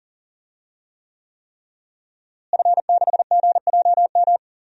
Morse code, heard as a keyed tone, about 40 wpm, 700 Hz. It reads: F6GJM
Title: DXCC-100 Call Signs - 5 Characters - Repeated 3x - 40wpm